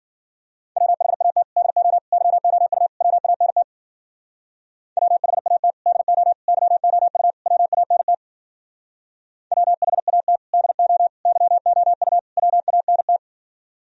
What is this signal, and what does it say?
Morse code, heard as a keyed tone, about 50 wpm, 700 Hz. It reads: WHAT DO YOU WANT WHAT DO YOU WANT WHAT DO YOU WANT
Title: Speed-Racing - Sets of 4 Words 40wpm